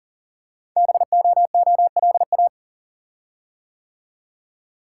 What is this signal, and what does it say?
Morse code, heard as a keyed tone, about 41 wpm, 700 Hz. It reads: BOOLA
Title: Code Groups - 5 Letters - 40wpm